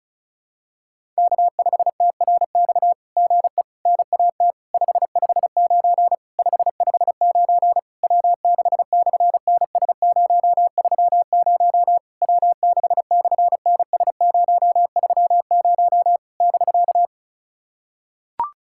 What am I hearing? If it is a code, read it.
Morse code, heard as a keyed tone, about 33 wpm, 700 Hz. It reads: K5TRX GE NAT 559 559 W6/NS030 W6/NS030 <BK>
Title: SOTA QSO Practice 35wpm